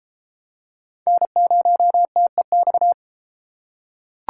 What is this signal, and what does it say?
Morse code, heard as a keyed tone, about 31 wpm, 700 Hz. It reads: N0TEX